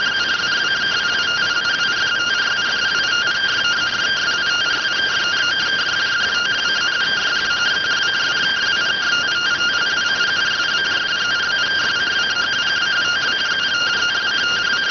Two-tone teleprinter signal, about 170 Hz shift.